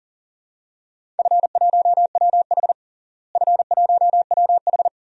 Morse code, heard as a keyed tone, about 40 wpm, 700 Hz. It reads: F1WH F1WH
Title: DXCC-100 Call Signs - 4 Characters - Repeated 3x - 40wpm